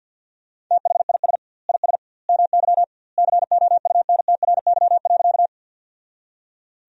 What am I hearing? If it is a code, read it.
Morse code, heard as a keyed tone, about 50 wpm, 700 Hz. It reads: THIS IS MY COUNTRY.
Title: 2x Word-Spacing - Sentences from Top 300 Words - 50wpm